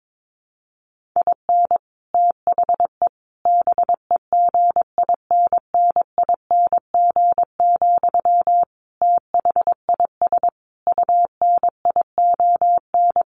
Morse code, heard as a keyed tone, about 22 wpm, 700 Hz. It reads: IN THE BEGINNING, THIS UNION
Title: Sentences from Top 1400 Words 22wpm